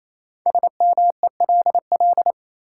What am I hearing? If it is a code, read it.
Morse code, heard as a keyed tone, about 28 wpm, 700 Hz. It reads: SMELL
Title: Five Letter Words - 28wpm